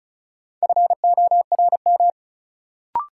Morse code, heard as a keyed tone, about 36 wpm, 700 Hz. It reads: FORM
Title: Sentences from Top 1000 Words - Review Part 1 - New Word Drill - 35wpm